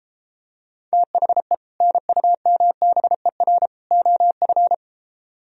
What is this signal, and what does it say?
Morse code, heard as a keyed tone, about 34 wpm, 700 Hz. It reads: THE NUMBER OF